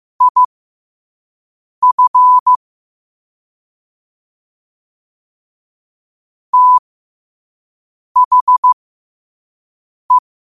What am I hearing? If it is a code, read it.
Morse code, heard as a keyed tone, about 15 wpm, spaced as for 5 wpm, 1000 Hz. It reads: IF THE
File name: qotdfiles / 5wpm